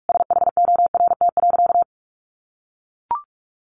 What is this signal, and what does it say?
Morse code, heard as a keyed tone, about 44 wpm, 700 Hz. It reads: SHORT.
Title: Sentences from Top 1400 Words 45wpm